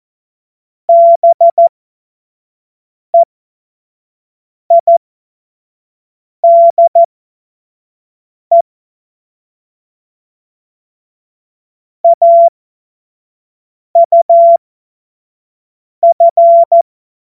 Morse code, heard as a keyed tone, about 14 wpm, spaced as for 5 wpm, 680 Hz. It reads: BEIDE AUF
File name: Morsekurs / Witze / Witz 20 Mai 2025 25 F70